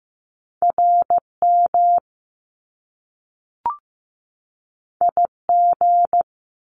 Morse code, heard as a keyed tone, about 15 wpm, 700 Hz. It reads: RM IG
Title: Top Two Letter Combinations in Words 15wpm